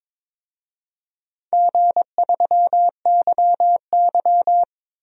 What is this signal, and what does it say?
Morse code, heard as a keyed tone, about 22 wpm, 700 Hz. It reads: G3YY
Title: DXCC-100 Call Signs - 4 Characters - 22wpm